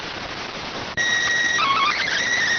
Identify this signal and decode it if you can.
SSTV (slow-scan television), PD 290.